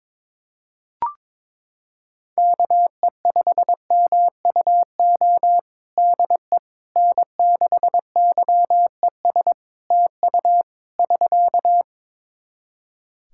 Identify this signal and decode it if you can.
Morse code, heard as a keyed tone, about 22 wpm, 700 Hz. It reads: KE5MUO DE N6YES TU <SK>
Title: K1USN SST QSO Practice 22wpm